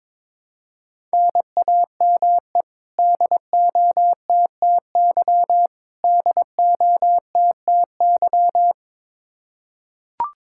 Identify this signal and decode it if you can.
Morse code, heard as a keyed tone, about 22 wpm, 700 Hz. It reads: NAME DOTTY DOTTY
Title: Practice QSO - Exchange 1 - Beginning - 22wpm